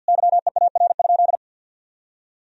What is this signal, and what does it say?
Morse code, heard as a keyed tone, about 49 wpm, 700 Hz. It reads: YEAR?
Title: Sentences from Top 1000 Words - Review Part 3 - 50wpm